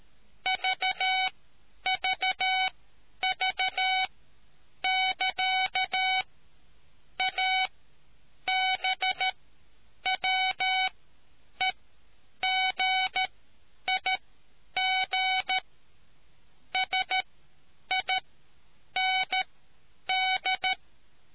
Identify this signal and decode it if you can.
Morse code, heard as a keyed tone, about 13 wpm, spaced as for 8 wpm, 750 Hz. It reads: VVV <KA> ABWEGIG SIND